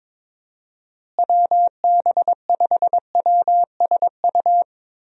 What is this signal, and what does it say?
Morse code, heard as a keyed tone, about 22 wpm, 700 Hz. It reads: WB5WSU
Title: USA Call Signs - 6 Characters - Repeated 2x - 22wpm